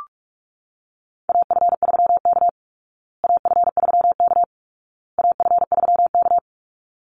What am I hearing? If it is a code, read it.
Morse code, heard as a keyed tone, about 44 wpm, 700 Hz. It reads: AF3K AF3K AF3K
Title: Call Signs Repeated 2x - 45wpm